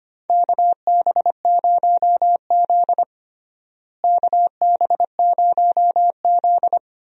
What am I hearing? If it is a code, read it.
Morse code, heard as a keyed tone, about 25 wpm, 700 Hz. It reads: KB0Z KB0Z
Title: USA Call Signs - 4 Characters - Repeated 2x - 25wpm